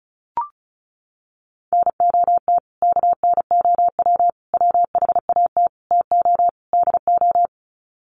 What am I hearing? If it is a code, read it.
Morse code, heard as a keyed tone, about 34 wpm, 700 Hz. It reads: NOT KNOW WHAT TO DO
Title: Sets of 5 Words 35wpm